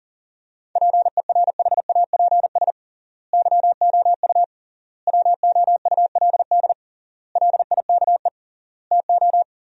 Morse code, heard as a keyed tone, about 40 wpm, 700 Hz. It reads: PERHAPS YOU WOULD LIKE TO
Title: 3x Word-Spacing - Sentences from Top 500 Words - 40wpm